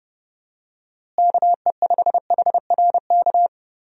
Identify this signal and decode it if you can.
Morse code, heard as a keyed tone, about 30 wpm, 700 Hz. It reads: KE5HRK
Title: USA Call Signs - 6 Characters - Repeated 2x - 30wpm